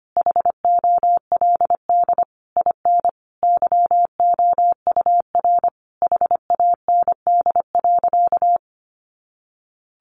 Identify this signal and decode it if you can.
Morse code, heard as a keyed tone, about 25 wpm, 700 Hz. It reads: HOLD IN YOUR HAND.
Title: Sentences from Top 400 Words 25wpm